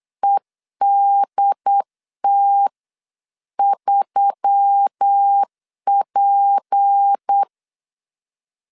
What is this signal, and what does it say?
Morse code, heard as a keyed tone, about 8 wpm, 800 Hz. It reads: EDT3P
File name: LEZIONI / 6 PKX0 / 6 P 8wpm 3025